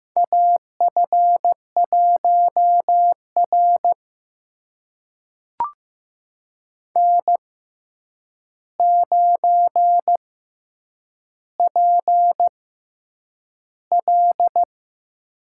Morse code, heard as a keyed tone, about 15 wpm, spaced as for 8 wpm, 700 Hz. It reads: AF1R N9PL
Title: USA Call Signs - 4 Characters - 6x Character Spacing - 15wpm